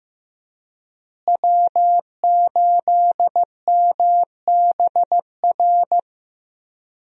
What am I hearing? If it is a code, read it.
Morse code, heard as a keyed tone, about 15 wpm, 700 Hz. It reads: W8MBR
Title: USA Call Signs - 5 Characters - Repeated 2x - 15wpm